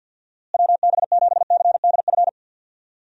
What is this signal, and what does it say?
Morse code, heard as a keyed tone, about 48 wpm, 700 Hz. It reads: WB7CDF
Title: USA Call Signs - 6 Characters - Repeated 3x - 50wpm